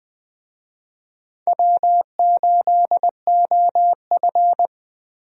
Morse code, heard as a keyed tone, about 20 wpm, 700 Hz. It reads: W8OF
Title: USA Call Signs - 4 Characters - 20wpm